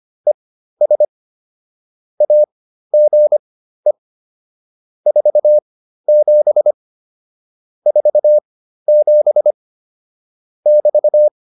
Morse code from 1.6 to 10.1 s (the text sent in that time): AGE 47 47